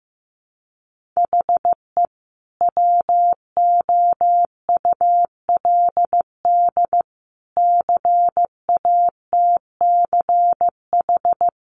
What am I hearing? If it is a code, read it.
Morse code, heard as a keyed tone, about 15 wpm, 700 Hz. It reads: HE WOULD CATCH
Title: Sentences from Top 1500 Words 15wpm